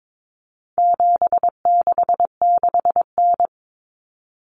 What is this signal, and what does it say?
Morse code, heard as a keyed tone, about 22 wpm, 700 Hz. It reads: 766N
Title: Common Cut Numbers - 4-Digit 22wpm